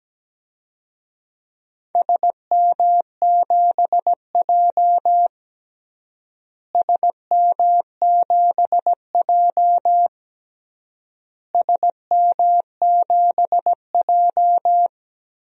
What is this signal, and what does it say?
Morse code, heard as a keyed tone, about 17 wpm, 700 Hz. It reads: SM7J SM7J SM7J
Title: DXCC-100 Call Signs - 4 Characters - Repeated 3x - 17wpm